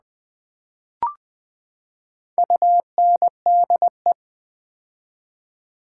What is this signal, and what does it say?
Morse code, heard as a keyed tone, about 20 wpm, 700 Hz. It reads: UNDE
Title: Build A Word - 4 Letters At A Time - 10-Letter Words - 20wpm